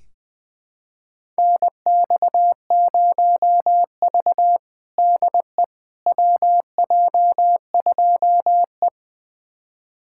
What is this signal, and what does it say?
Morse code, heard as a keyed tone, about 21 wpm, 700 Hz. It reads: NX0V DE WJ2E
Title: USA Callsign DE Callsign - 4 Characters - 20wpm